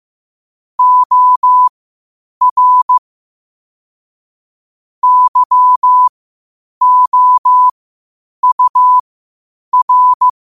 Morse code, heard as a keyed tone, about 15 wpm, spaced as for 8 wpm, 1000 Hz. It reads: OR YOUR
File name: qotdfiles / 8wpm